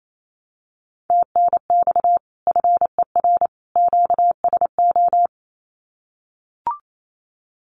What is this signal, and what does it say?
Morse code, heard as a keyed tone, about 28 wpm, 700 Hz. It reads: TNX FER QSO
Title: Common QSO Phrases 28wpm